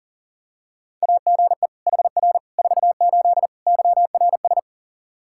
Morse code, heard as a keyed tone, about 40 wpm, 700 Hz. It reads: AGE HR 48 YRS
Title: K1USN SST QSO Practice 40wpm